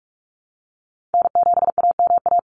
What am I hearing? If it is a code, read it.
Morse code, heard as a keyed tone, about 44 wpm, 700 Hz. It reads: N7AMA
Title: NA SOTA Chaser Callsigns 45wpm